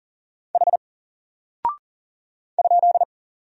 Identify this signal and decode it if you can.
Morse code, heard as a keyed tone, about 39 wpm, 700 Hz. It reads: H ?
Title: Single Character (Letter:Number:Punctuation:Prosign) Alt 40wpm